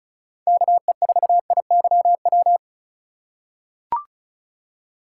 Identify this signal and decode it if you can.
Morse code, heard as a keyed tone, about 34 wpm, 700 Hz. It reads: KE4IYW
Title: USA Call Signs - 2x Character Spacing - 35wpm